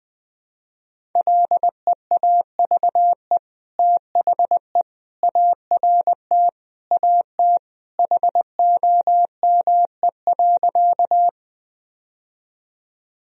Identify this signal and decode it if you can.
Morse code, heard as a keyed tone, about 19 wpm, 700 Hz. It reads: LEAVE THE ART AT HOME.
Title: Sentences from Top 1000 Words - Review Part 3 - 20wpm